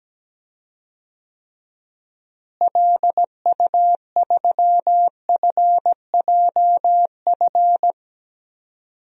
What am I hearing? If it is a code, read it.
Morse code, heard as a keyed tone, about 17 wpm, 700 Hz. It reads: LU3FJF
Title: DXCC-100 Call Signs - 6 Characters - Repeated 2x - 17wpm